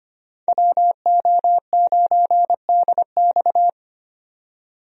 Morse code, heard as a keyed tone, about 25 wpm, 690 Hz. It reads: WO9DX
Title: USA Call Signs - 5 Characters - 25wpm